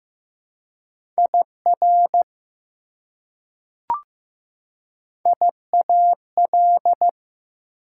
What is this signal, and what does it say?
Morse code, heard as a keyed tone, about 15 wpm, 700 Hz. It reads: IR IAL
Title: Prefixes and Suffixes 15wpm